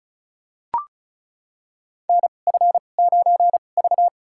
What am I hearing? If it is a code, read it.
Morse code, heard as a keyed tone, about 36 wpm, spaced as for 25 wpm, 700 Hz. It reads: NF9V
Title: USA Call Signs - 4 Characters - 2x Character Spacing - 35wpm